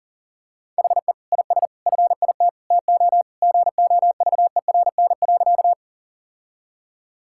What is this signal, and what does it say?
Morse code, heard as a keyed tone, about 40 wpm, 700 Hz. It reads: HE IS FIT TO GOVERN.
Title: Sentences from Top 1000 Words - Review Part 5 - 40wpm